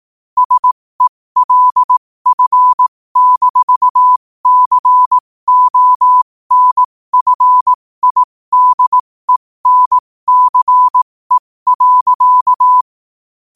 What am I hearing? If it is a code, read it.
Morse code, heard as a keyed tone, about 18 wpm, spaced as for 15 wpm, 1000 Hz. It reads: SELF-CONFIDENCE.
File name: qotdfiles / 15wpm